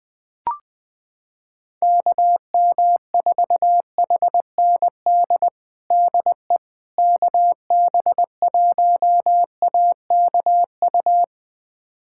Morse code, heard as a keyed tone, about 20 wpm, 700 Hz. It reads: KM4HND DE KB1AKU
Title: USA Callsign DE Callsign 20wpm